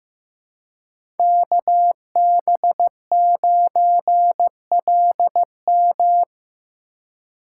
Morse code, heard as a keyed tone, about 15 wpm, 700 Hz. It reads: KB9LM
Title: USA Call Signs - 5 Characters - 4x Character Spacing - 15wpm